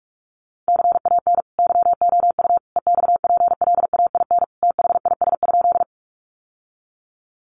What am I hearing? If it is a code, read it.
Morse code, heard as a keyed tone, about 44 wpm, 700 Hz. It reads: CAN YOU EXPLAIN THIS?
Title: Sentences from Top 1100 Words 45wpm